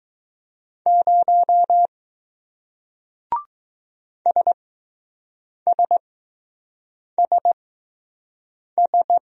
Morse code, heard as a keyed tone, about 20 wpm, spaced as for 6 wpm, 700 Hz. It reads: 0 SSSS